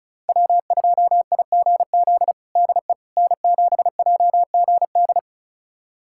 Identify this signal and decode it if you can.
Morse code, heard as a keyed tone, about 34 wpm, 700 Hz. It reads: W2IGZ DE N7JGD
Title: USA Callsign DE Callsign - 5 Characters - 35wpm